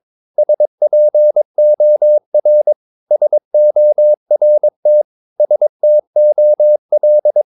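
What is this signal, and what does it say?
Morse code, heard as a keyed tone, about 22 wpm, 600 Hz. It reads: SPOR SORT STOL
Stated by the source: Morse practice of short Danish words at 22 wpm, from CW med Gnister.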